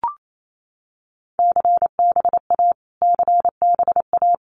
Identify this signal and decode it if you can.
Morse code, heard as a keyed tone, about 28 wpm, 700 Hz. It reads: CBA CBA
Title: Abbreviations Repeated 3x - 28wpm